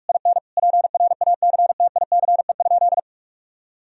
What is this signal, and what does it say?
Morse code, heard as a keyed tone, about 48 wpm, 700 Hz. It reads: IN PRACTICE?
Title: Sentences from Top 1000 Words - Review Part 4 - 45wpm